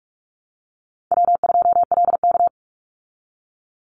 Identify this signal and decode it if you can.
Morse code, heard as a keyed tone, about 44 wpm, 700 Hz. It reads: W2LK
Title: Call Signs Repeated 3x - 45wpm